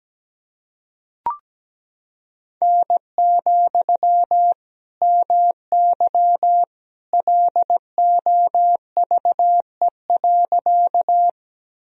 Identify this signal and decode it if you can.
Morse code, heard as a keyed tone, about 17 wpm, 700 Hz. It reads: N, MY LOVE.
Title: Sentences from Top 400 Words and Abbreviations 17wpm